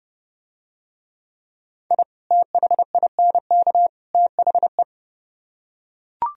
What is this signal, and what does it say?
Morse code, heard as a keyed tone, about 30 wpm, 700 Hz. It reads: I THINK THE